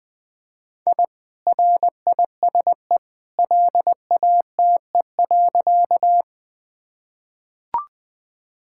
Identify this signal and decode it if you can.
Morse code, heard as a keyed tone, about 20 wpm, 700 Hz. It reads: I RISE LATE.
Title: Sentences from Top 1000 Words - Review Part 6 - 20wpm